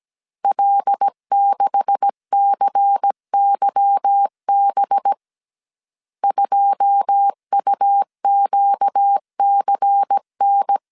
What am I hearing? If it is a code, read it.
Morse code, heard as a keyed tone, about 17 wpm, 800 Hz. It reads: L6CYB 2UQCN
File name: CW / DET / 1 9 1500C S C 16wpm 2909